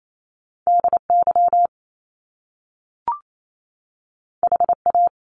Morse code, heard as a keyed tone, about 28 wpm, 700 Hz. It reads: D Y H A
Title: Top Two Letter Combinations in Words 28wpm